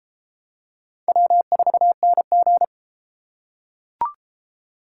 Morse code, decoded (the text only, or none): W4NG